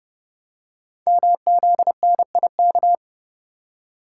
Morse code, heard as a keyed tone, about 30 wpm, 690 Hz. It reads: MZNIK
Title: Code Groups - 5 Letters - 30wpm